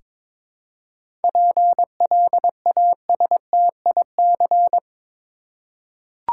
Morse code, heard as a keyed tone, about 22 wpm, 700 Hz. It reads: PLASTIC
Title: Seven Letter Words - 22wpm